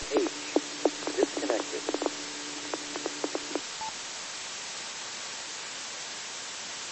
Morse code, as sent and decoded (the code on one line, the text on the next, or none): none